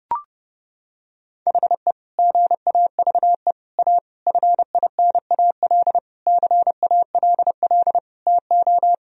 Morse code, heard as a keyed tone, about 30 wpm, 700 Hz. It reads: HE GAVE A FINAL CALL TO
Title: Sentences from Top 1000 Words - Review Part 2 - 30wpm